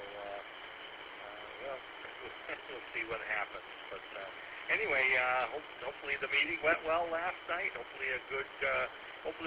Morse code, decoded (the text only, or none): none